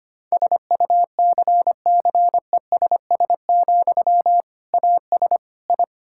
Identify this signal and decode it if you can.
Morse code, heard as a keyed tone, about 25 wpm, 700 Hz. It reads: SUCCESS, AS I
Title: Sentences from Top 1000 Words - Review Part 6 - 25wpm